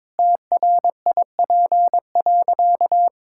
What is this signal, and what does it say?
Morse code, heard as a keyed tone, about 22 wpm, 700 Hz. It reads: TRIP.
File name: practice / 22wpm / Sentences with 2-Word Phrases - 22wpm